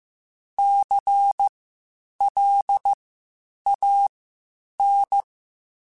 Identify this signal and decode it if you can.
Morse code, heard as a keyed tone, about 15 wpm, spaced as for 8 wpm, 800 Hz. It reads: CLAN
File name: Spaced Alphas / Spaced Alphas mp3s / Cs sp @13